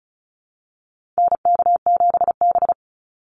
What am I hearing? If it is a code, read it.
Morse code, heard as a keyed tone, about 34 wpm, 700 Hz. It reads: NK7B
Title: Call Signs Repeated 2x - 35wpm